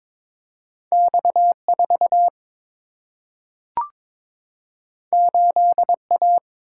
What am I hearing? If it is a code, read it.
Morse code, heard as a keyed tone, about 22 wpm, 700 Hz. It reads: X4 8A
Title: Single-Letter Single-Number 22wpm